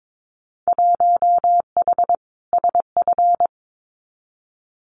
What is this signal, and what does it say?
Morse code, heard as a keyed tone, about 22 wpm, 700 Hz. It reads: 1H SF
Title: Winter Field Day Exchanges 22wpm